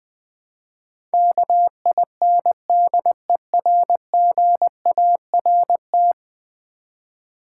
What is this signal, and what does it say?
Morse code, heard as a keyed tone, about 20 wpm, 700 Hz. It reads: KINDERGART
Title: Build A Word - 5 Letters At A Time - 12-Letter Words - 20wpm